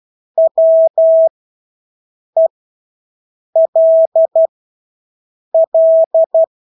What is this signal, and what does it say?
Morse code, heard as a keyed tone, about 12 wpm, spaced as for 6 wpm, 650 Hz. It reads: WELL
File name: Engelske ord 12-6 wpm